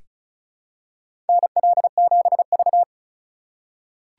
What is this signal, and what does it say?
Morse code, heard as a keyed tone, about 34 wpm, 700 Hz. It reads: NL7V
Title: K1USN SST QSO Practice - 35wpm